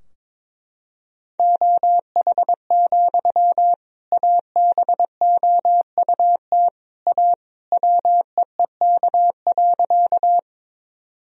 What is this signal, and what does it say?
Morse code, heard as a keyed tone, about 22 wpm, 700 Hz. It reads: OH, ABOUT A WEEK.
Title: Sentences from Top 1000 Words - Review Part 2 - 22wpm